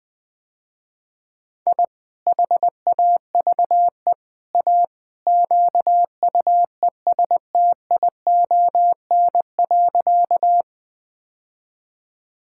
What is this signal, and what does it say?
Morse code, heard as a keyed tone, about 20 wpm, 700 Hz. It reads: I HAVE A QUESTION.
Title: Sentences with 3-Word Phrases - Spoken Beginning - 20wpm